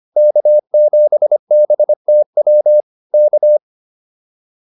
Morse code, heard as a keyed tone, about 25 wpm, 600 Hz. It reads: K7BTW K